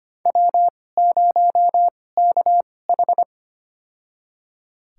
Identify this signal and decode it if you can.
Morse code, heard as a keyed tone, about 25 wpm, spaced as for 18 wpm, 700 Hz. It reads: W0KH
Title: USA Call Signs - 4 Characters - 2x Character Spacing - 25wpm